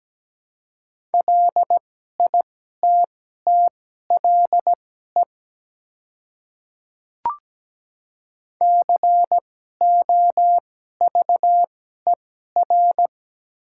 Morse code, heard as a keyed tone, about 17 wpm, spaced as for 12 wpm, 690 Hz. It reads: LITTLE COVER
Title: 2x Character Spacing - Top 300 Words - 17wpm